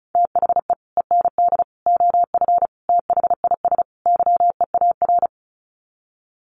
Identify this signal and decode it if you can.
Morse code, heard as a keyed tone, about 34 wpm, 700 Hz. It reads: THE END OF THIS YEAR
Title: Sets of 5 Words Encore 35wpm